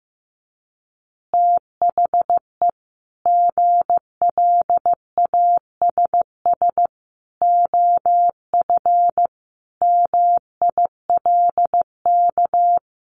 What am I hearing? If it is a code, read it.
Morse code, heard as a keyed tone, about 15 wpm, 700 Hz. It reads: THE GLASS OF MILK
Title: Additional Easy Phrases 15wpm